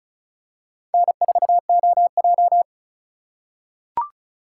Morse code, heard as a keyed tone, about 36 wpm, 700 Hz. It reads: N4OJ
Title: USA Call Signs - 4 Characters - 3x Character Spacing - 35wpm